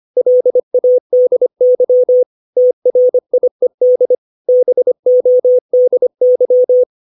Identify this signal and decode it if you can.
Morse code, heard as a keyed tone, about 25 wpm, 500 Hz. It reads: LADY TRIED BODY
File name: 632 words / 632 Words 25WPM